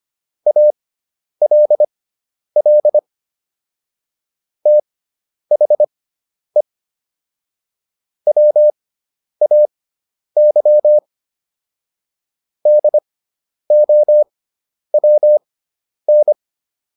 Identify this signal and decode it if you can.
Morse code, heard as a keyed tone, about 25 wpm, spaced as for 10 wpm, 600 Hz. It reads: ALL THE WAY DOWN